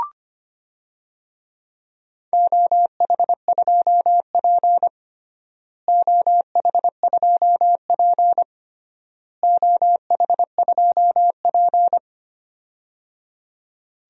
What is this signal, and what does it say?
Morse code, heard as a keyed tone, about 25 wpm, 700 Hz. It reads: OH2P OH2P OH2P
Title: DXCC-100 Call Signs - 4 Characters - Repeated 3x - 25wpm